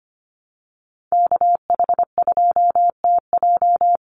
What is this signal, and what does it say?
Morse code, heard as a keyed tone, about 25 wpm, 690 Hz. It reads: KH2TJ
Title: NA SOTA Activator Callsigns 25wpm